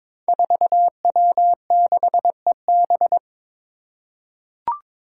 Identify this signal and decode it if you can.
Morse code, heard as a keyed tone, about 21 wpm, 700 Hz. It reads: 4W6EB
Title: Five Challenging Letter-Number Combinations 22wpm